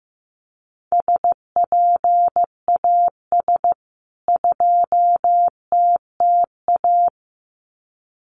Morse code, heard as a keyed tone, about 15 wpm, 700 Hz. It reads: SPAS 2TTA
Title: CWT Exchange 15wpm